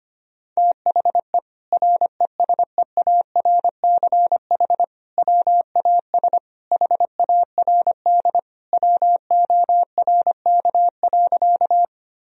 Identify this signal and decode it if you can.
Morse code, heard as a keyed tone, about 25 wpm, 700 Hz. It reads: THE RESEARCH WAS HARD WORK.
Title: Sentences from Top 1000 Words - Review Part 4 - 25wpm